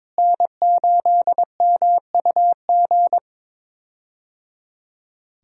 Morse code, heard as a keyed tone, about 22 wpm, 700 Hz. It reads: N8MUG
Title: USA Call Signs - 5 Characters - Repeated 3x - 22wpm